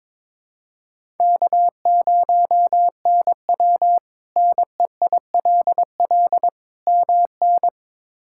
Morse code, heard as a keyed tone, about 21 wpm, 700 Hz. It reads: K0NW NEILL MN
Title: K1USN SST QSO Practice - 22wpm